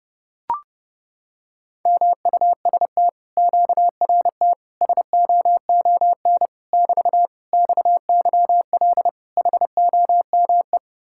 Morse code, heard as a keyed tone, about 32 wpm, 700 Hz. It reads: MUST QRT SOON <BT> XYL HOME